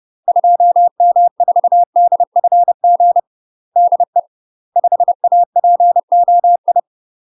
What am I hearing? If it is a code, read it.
Morse code, heard as a keyed tone, about 30 wpm, 700 Hz. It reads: 2M4DFG DE 5APOI
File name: UK MASTER QSOs TRAINING / 30WPM / UK MASTER QSOs 0049